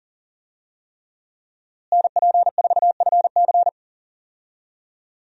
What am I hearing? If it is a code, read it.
Morse code, heard as a keyed tone, about 38 wpm, 700 Hz. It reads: NP4FC
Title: USA Call Signs - 5 Characters - Repeated 3x - 40wpm